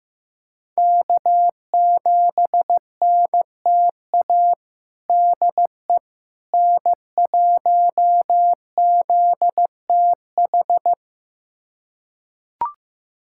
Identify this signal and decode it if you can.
Morse code, heard as a keyed tone, about 15 wpm, 700 Hz. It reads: K7NTA DE N1ZTH
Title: USA Callsign DE Callsign - 5 Characters - 15wpm